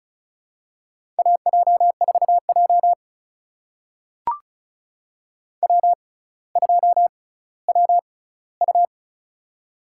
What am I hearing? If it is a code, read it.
Morse code, heard as a keyed tone, about 36 wpm, spaced as for 18 wpm, 700 Hz. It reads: AJ4J W2WU
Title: USA Call Signs - 4 Characters - 6x Character Spacing - 35wpm